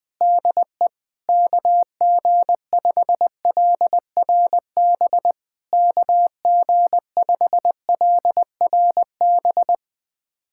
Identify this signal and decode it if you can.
Morse code, heard as a keyed tone, about 20 wpm, 700 Hz. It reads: DE KG5LRB KG5LRB
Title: POTA QSO Practice - 20wpm